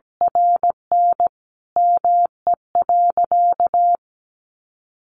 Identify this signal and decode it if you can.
Morse code, heard as a keyed tone, about 17 wpm, 700 Hz. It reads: RN ME.
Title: Sentences from Top 1000 Words 17wpm